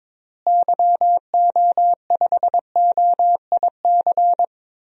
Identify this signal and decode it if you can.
Morse code, heard as a keyed tone, about 22 wpm, 700 Hz. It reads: YO5OIC
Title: DXCC-100 Call Signs - 6 Characters - Repeated 2x - 22wpm